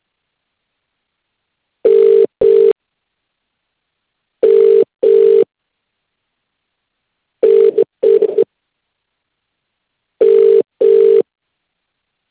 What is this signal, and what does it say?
Ringback tone (UK), 400 + 450 Hz.